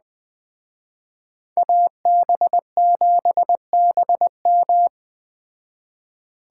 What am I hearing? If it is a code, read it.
Morse code, heard as a keyed tone, about 20 wpm, 700 Hz. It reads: AB7BM